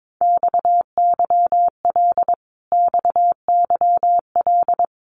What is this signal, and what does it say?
Morse code, heard as a keyed tone, about 22 wpm, 700 Hz. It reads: XYL XYL
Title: Abbreviations Repeated 3x - 22wpm